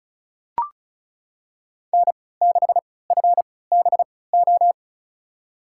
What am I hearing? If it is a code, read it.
Morse code, heard as a keyed tone, about 36 wpm, spaced as for 20 wpm, 690 Hz. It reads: N6FBO
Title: USA Call Signs - 5 Characters - 3x Character Spacing - 35wpm